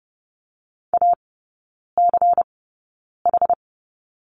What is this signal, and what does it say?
Morse code, heard as a keyed tone, about 30 wpm, spaced as for 9 wpm, 700 Hz. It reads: ACH
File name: Lesson 16e - Letter c - Three Characters 30wpm